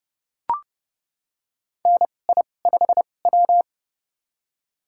Morse code, heard as a keyed tone, about 32 wpm, spaced as for 22 wpm, 700 Hz. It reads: NI5W
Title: USA Call Signs - 4 Characters - 2x Character Spacing - 30wpm